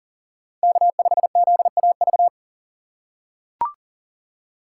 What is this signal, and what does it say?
Morse code, heard as a keyed tone, about 40 wpm, 700 Hz. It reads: K5ZAV
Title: USA Call Signs - 4x Character Spacing - 40wpm